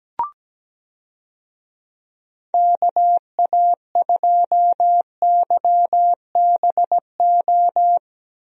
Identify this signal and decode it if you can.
Morse code, heard as a keyed tone, about 17 wpm, 700 Hz. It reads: KA2YBO
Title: USA Call Signs - 6 Characters - Repeated 2x - 17wpm